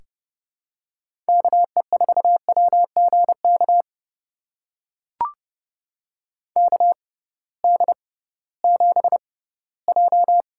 Morse code, heard as a keyed tone, about 31 wpm, spaced as for 18 wpm, 700 Hz. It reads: KE4WGK KD7J
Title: USA Call Signs - 6 Characters - 6x Character Spacing - 30wpm